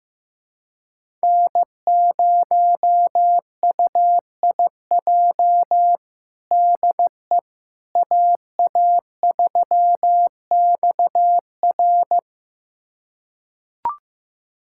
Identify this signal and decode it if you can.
Morse code, heard as a keyed tone, about 15 wpm, 700 Hz. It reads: N0UIJ DE AA3XR
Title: USA Callsign DE Callsign - 5 Characters - 15wpm